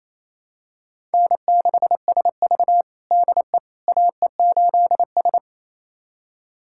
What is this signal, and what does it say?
Morse code, heard as a keyed tone, about 28 wpm, 700 Hz. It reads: N6SV DE AE8S
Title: USA Callsign DE Callsign - 4 Characters - 28wpm